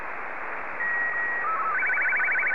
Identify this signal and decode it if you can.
SSTV (slow-scan television), Martin 2.